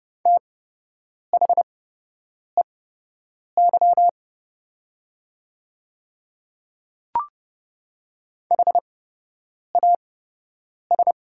Morse code, heard as a keyed tone, about 30 wpm, spaced as for 8 wpm, 700 Hz. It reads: THEY HAS
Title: 8x Character Spacing - Top 100 Words - 30wpm